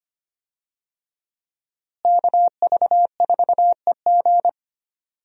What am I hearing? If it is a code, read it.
Morse code, heard as a keyed tone, about 25 wpm, 700 Hz. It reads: KV4EG
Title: USA Call Signs - 5 Characters - Repeated 3x - 25wpm